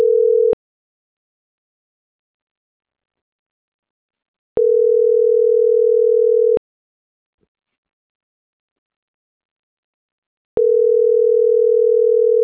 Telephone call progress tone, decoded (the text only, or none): ringback tone